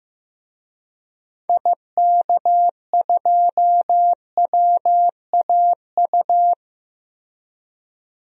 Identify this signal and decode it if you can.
Morse code, heard as a keyed tone, about 15 wpm, 700 Hz. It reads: IK2WAU
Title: DXCC-100 Call Signs - 6 Characters - 15wpm